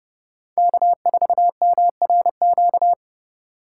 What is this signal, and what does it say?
Morse code, heard as a keyed tone, about 30 wpm, 700 Hz. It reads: K4MRQ